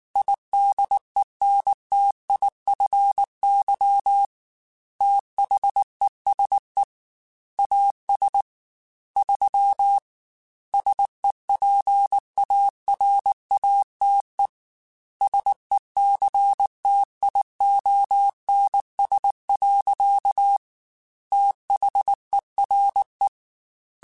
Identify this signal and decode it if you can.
Morse code, heard as a keyed tone, about 19 wpm, 800 Hz. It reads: IDENTIFY THESE AS 3 SEPARATE SECTIONS. THERE
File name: Technical Subjects / Technical Subjects-mp3s / Sw Pwr Sup @18